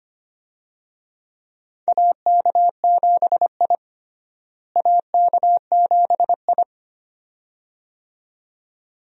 Morse code, heard as a keyed tone, about 25 wpm, 700 Hz. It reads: AK7I AK7I
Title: USA Call Signs - 4 Characters - Repeated 2x - 25wpm